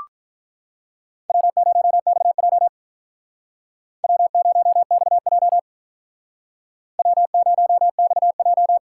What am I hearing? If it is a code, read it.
Morse code, heard as a keyed tone, about 45 wpm, 700 Hz. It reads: W0XJ W0XJ W0XJ